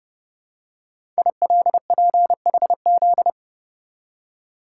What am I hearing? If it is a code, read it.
Morse code, heard as a keyed tone, about 28 wpm, 700 Hz. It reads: ILPHZ
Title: Code Groups - 5 Letter:Numbers - 30wpm